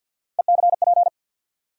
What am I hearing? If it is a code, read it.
Morse code, heard as a keyed tone, about 50 wpm, 700 Hz. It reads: ECP
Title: USA Call Sign Postfixes - 3 Letters 50wpm